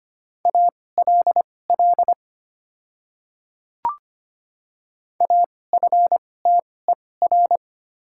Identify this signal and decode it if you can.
Morse code, heard as a keyed tone, about 25 wpm, spaced as for 18 wpm, 700 Hz. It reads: ALL AFTER
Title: 2x Character Spacing - Top 200 Words - 25wpm